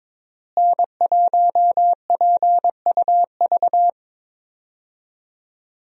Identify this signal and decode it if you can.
Morse code, heard as a keyed tone, about 22 wpm, 700 Hz. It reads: N1PUV
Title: USA Call Signs - 5 Characters - Repeated 2x - 22wpm